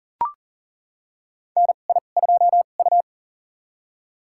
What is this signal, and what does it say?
Morse code, heard as a keyed tone, about 37 wpm, spaced as for 28 wpm, 700 Hz. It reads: NI2U
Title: USA Call Signs - 4 Characters - 2x Character Spacing - 40wpm